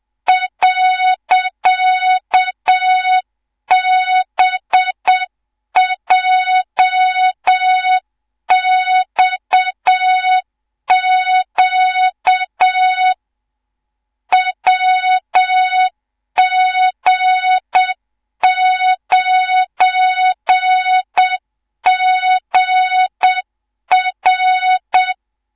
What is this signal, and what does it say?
Morse code, heard as a keyed tone, about 7 wpm, 3700 Hz, with noise. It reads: .BJXQ WG9GR
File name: Morze-HA2MN / hang / 18 IRASJELEK2